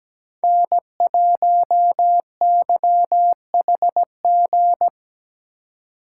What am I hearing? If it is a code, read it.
Morse code, heard as a keyed tone, about 17 wpm, 700 Hz. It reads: N1YHG
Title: USA Call Signs - 5 Characters - 4x Character Spacing - 17wpm